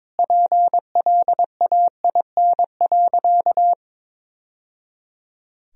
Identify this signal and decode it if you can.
Morse code, heard as a keyed tone, about 22 wpm, 700 Hz. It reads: PLAIN.
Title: Sentences from Top 1000 Words - Review Part 2 - 22wpm